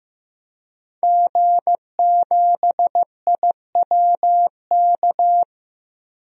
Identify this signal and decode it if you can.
Morse code, heard as a keyed tone, about 15 wpm, 700 Hz. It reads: G7IWK
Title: DXCC-100 Call Signs - 5 Characters - 15wpm